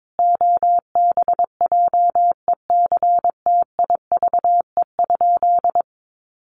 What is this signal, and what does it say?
Morse code, heard as a keyed tone, about 22 wpm, 700 Hz. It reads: OBJECTIVE?
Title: Sentences from Top 1200 Words 22wpm